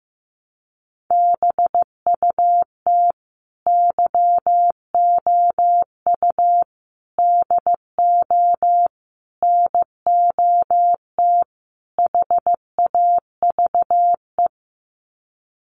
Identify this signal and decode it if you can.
Morse code, heard as a keyed tone, about 15 wpm, 700 Hz. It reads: BUT YOU DO NOT HAVE
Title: Sets of 5 Words Encore 15wpm